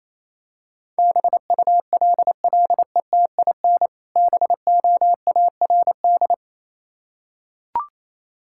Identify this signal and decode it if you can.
Morse code, heard as a keyed tone, about 28 wpm, 700 Hz. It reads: BULLETIN BOARD